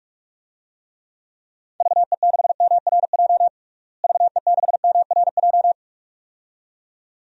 Morse code, heard as a keyed tone, about 41 wpm, 700 Hz. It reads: VE6MRJ VE6MRJ
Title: DXCC-100 Call Signs - 6 Characters - Repeated 2x - 45wpm